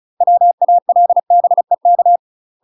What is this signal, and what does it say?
Morse code, heard as a keyed tone, about 36 wpm, 700 Hz. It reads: WALBEK
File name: UK MASTER QSOs TRAINING / 35WPM / UK MASTER QSOs 0038